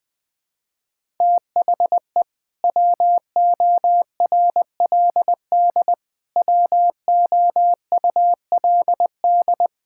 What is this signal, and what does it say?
Morse code, heard as a keyed tone, about 20 wpm, 700 Hz. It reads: THE WORLD WOULD
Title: Sets of 8 Words - 20wpm